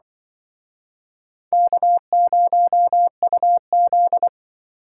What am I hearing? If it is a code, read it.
Morse code, heard as a keyed tone, about 24 wpm, 700 Hz. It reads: K0UZ
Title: Speed Racing - USA 1x2 Call Signs 20wpm